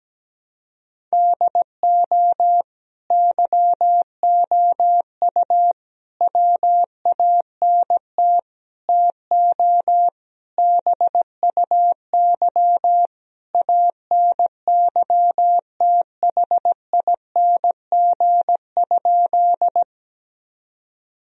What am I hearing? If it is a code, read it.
Morse code, heard as a keyed tone, about 17 wpm, 700 Hz. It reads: DO YOU WANT TO BUY ANYTHING?
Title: Sentences from Top 1000 Words - Review Part 4 - 17wpm